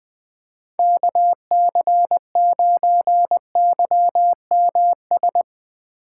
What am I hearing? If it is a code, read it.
Morse code, heard as a keyed tone, about 20 wpm, 700 Hz. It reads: KC9YMS
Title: USA Call Signs - 6 Characters - 4x Character Spacing - 20wpm